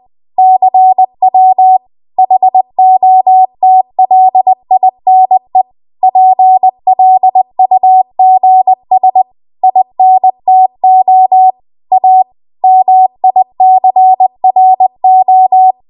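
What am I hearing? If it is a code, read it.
Morse code, heard as a keyed tone, about 20 wpm, 750 Hz. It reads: CW HOTLINE PLUGS INTO A MICRO
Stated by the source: Morse code practice at 20 WPM.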